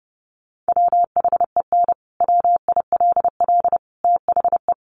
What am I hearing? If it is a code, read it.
Morse code, heard as a keyed tone, about 30 wpm, 700 Hz. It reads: WHEN WILL THE
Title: Sentences from Top 1300 Words 30wpm